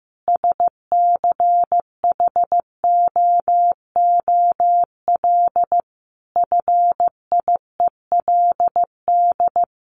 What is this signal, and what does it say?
Morse code, heard as a keyed tone, about 15 wpm, 700 Hz. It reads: SCHOOL FIELD
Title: Additional Easy Phrases 15wpm